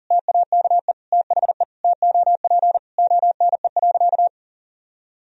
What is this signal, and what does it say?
Morse code, heard as a keyed tone, about 37 wpm, 700 Hz. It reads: TAKE THE TOP ONE.
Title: Sentences from Top 1000 Words - Review Part 2 - 40wpm